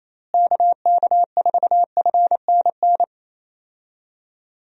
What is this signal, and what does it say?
Morse code, heard as a keyed tone, about 28 wpm, 700 Hz. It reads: KK4FNN
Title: USA Call Signs - 6 Characters - Repeated 2x - 28wpm